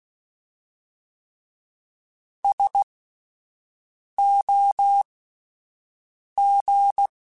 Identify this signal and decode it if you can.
Morse code, heard as a keyed tone, about 16 wpm, spaced as for 5 wpm, 800 Hz. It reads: SOG